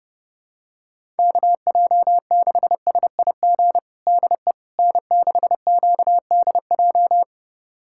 Morse code, heard as a keyed tone, about 30 wpm, 700 Hz. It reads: KJ6SIG DE N6QDJ
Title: USA Callsign DE Callsign 30wpm